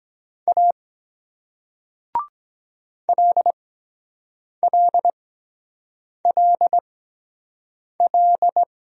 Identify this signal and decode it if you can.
Morse code, heard as a keyed tone, about 23 wpm, spaced as for 7 wpm, 700 Hz. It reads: A LLLL